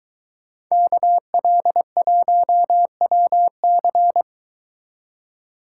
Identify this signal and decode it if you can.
Morse code, heard as a keyed tone, about 23 wpm, 700 Hz. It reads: KL1WC